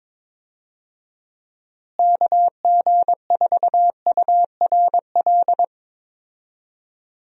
Morse code, heard as a keyed tone, about 22 wpm, 700 Hz. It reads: KG4URL
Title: USA Call Signs - 6 Characters - Repeated 2x - 22wpm